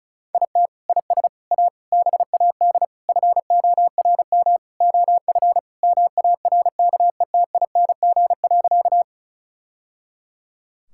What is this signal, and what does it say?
Morse code, heard as a keyed tone, about 36 wpm, 700 Hz. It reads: IT IS A BAD FORM OF MARKETING.
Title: Sentences from Top 1000 Words - Review Part 5 - 35wpm